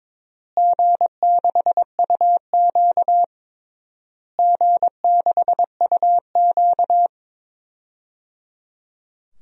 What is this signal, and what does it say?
Morse code, heard as a keyed tone, about 22 wpm, 700 Hz. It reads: G6UQ G6UQ
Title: DXCC-100 Call Signs - 4 Characters - Repeated 3x - 22wpm